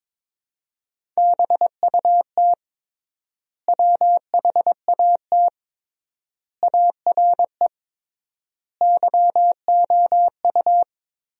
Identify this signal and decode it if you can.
Morse code, heard as a keyed tone, about 22 wpm, 700 Hz. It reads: BUT WHAT ARE YOU
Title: 3x Word-Spacing - Sentences from Top 100 Words - 22wpm